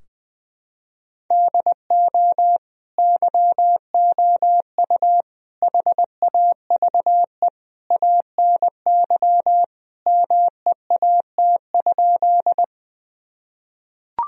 Morse code, heard as a keyed tone, about 21 wpm, 700 Hz. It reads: DO YOU HAVE ANY MEAT?
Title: Sentences from Top 1000 Words - Review Part 3 - 20wpm